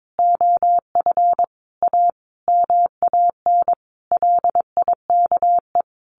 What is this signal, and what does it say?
Morse code, heard as a keyed tone, about 22 wpm, 700 Hz. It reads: OF A MAN LIKE
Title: Sentences from Top 1200 Words 22wpm